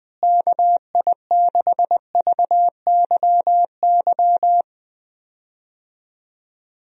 Morse code, heard as a keyed tone, about 20 wpm, 700 Hz. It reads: KI6VYY
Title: USA Call Signs - 6 Characters - Repeated 3x - 20wpm